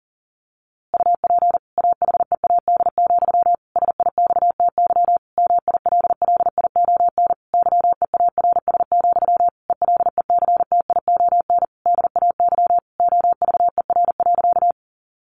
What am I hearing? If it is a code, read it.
Morse code, heard as a keyed tone, about 40 wpm, 700 Hz. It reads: UP AHEAD, SIXTY MILLION YEARS, ELECTION DAY OVER.
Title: A Sound Of Thunder 40wpm